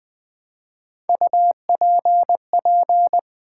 Morse code, heard as a keyed tone, about 20 wpm, 700 Hz. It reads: UPP
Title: Build A Word - 1 Letter At A Time - 5-Letter Words - 20wpm